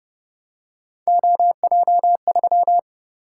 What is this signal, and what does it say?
Morse code, heard as a keyed tone, about 29 wpm, 700 Hz. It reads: OJ3